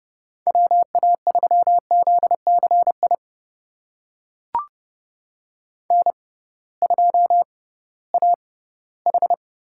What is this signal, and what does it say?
Morse code, heard as a keyed tone, about 30 wpm, spaced as for 17 wpm, 700 Hz. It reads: WA3ZCI N2AH
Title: USA Call Signs - 6x Character Spacing - 30wpm